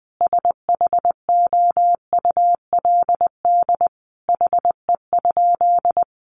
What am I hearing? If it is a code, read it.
Morse code, heard as a keyed tone, about 20 wpm, 700 Hz. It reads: SHOULD HE?
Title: Sentences from Top 300 Words 20wpm